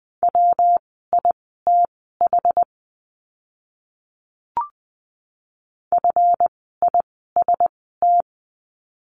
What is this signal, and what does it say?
Morse code, heard as a keyed tone, about 20 wpm, spaced as for 14 wpm, 700 Hz. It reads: WITH FIST
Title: 139 Lesson 23f - Letter f - Words 20wpm